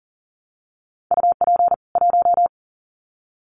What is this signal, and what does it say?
Morse code, heard as a keyed tone, about 40 wpm, 700 Hz. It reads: UP 1
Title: Common QSO Phrases 40wpm